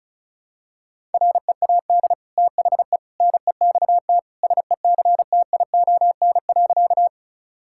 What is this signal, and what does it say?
Morse code, heard as a keyed tone, about 36 wpm, 700 Hz. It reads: READ THE NEXT SECTION.
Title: Sentences from Top 1000 Words - Review Part 5 - 35wpm